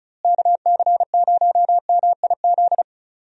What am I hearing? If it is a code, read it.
Morse code, heard as a keyed tone, about 36 wpm, 700 Hz. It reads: KC0MIZ
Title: USA Call Signs - 6 Characters - Repeated 2x - 35wpm